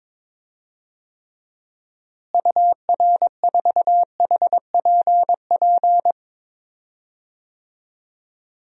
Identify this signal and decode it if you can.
Morse code, heard as a keyed tone, about 22 wpm, 700 Hz. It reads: UR4HPP
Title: DXCC-100 Call Signs - 6 Characters - Repeated 3x - 22wpm